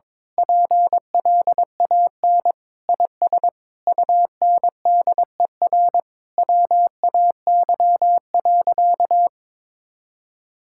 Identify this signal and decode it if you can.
Morse code, heard as a keyed tone, about 22 wpm, 700 Hz. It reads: PLAN IS UNDER WAY.
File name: Sentences with 9-Word Phrases - 22wpm